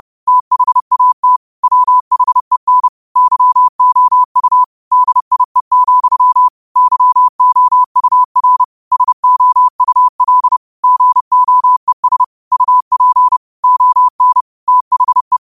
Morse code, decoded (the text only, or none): THAT WHEN YOU DIE, YOUR SOUL GOES UP ON THE